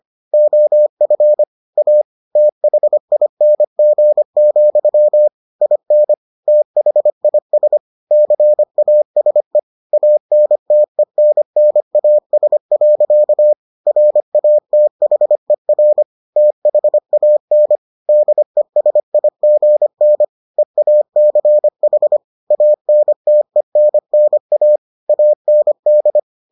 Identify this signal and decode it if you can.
Morse code, heard as a keyed tone, about 25 wpm, 600 Hz. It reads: OF A THING, IN THIS CASE ANTENNAS. RATHER THAN DESIGN EACH ANTENNA AND